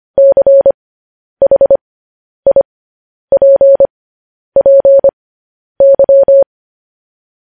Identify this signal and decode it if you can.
Morse code, heard as a keyed tone, about 25 wpm, spaced as for 10 wpm, 550 Hz. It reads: CHIPPY
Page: news headlines in morse code at 10 wpm